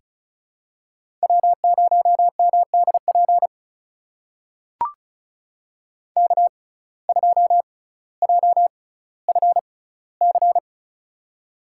Morse code, heard as a keyed tone, about 34 wpm, spaced as for 18 wpm, 700 Hz. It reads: W0MDP K2JFC
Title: USA Call Signs - 5 Characters - 6x Character Spacing - 35wpm